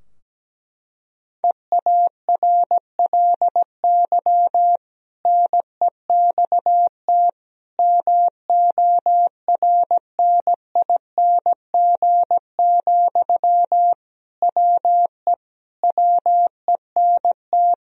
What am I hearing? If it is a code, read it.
Morse code, heard as a keyed tone, about 17 wpm, 700 Hz. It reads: EARLY NEXT MORNING, WE WENT
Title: Sentences from Top 1000 Words - Review Part 4 - 17wpm